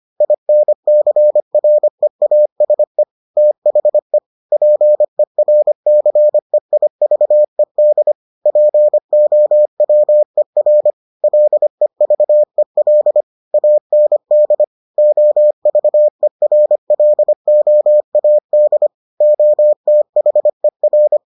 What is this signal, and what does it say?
Morse code, heard as a keyed tone, about 25 wpm, 600 Hz. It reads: INCREASE THE PERCEIVED POWER LEVEL AND OVERLOAD OTHER